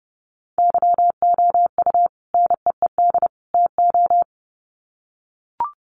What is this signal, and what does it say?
Morse code, heard as a keyed tone, about 30 wpm, 700 Hz. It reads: YOU NEED TO
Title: Sets of 3 Words 30wpm